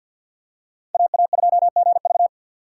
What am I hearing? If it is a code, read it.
Morse code, heard as a keyed tone, about 51 wpm, 700 Hz. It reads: AA2GV
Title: USA Call Signs - 4x Character Spacing - 50wpm